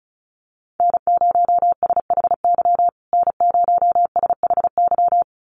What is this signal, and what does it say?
Morse code, heard as a keyed tone, about 34 wpm, 700 Hz. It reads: N0SHY N0SHY
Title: USA Call Signs Repeated 2x 35wpm